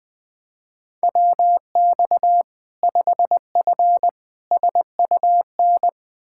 Morse code, heard as a keyed tone, about 20 wpm, 700 Hz. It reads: WX 5F SUN
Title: Practice QSO - Exchange 2 - Info - 20wpm